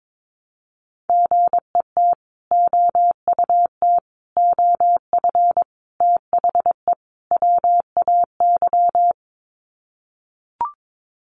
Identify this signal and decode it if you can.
Morse code, heard as a keyed tone, about 22 wpm, 700 Hz. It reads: GET OUT OF THE WAY
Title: Sets of 5 Words Encore 22wpm